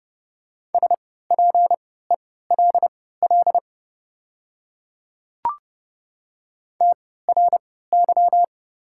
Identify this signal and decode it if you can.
Morse code, heard as a keyed tone, about 31 wpm, spaced as for 17 wpm, 700 Hz. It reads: SPELL TRY
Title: 3x Character Spacing - Top 500 Words - 30wpm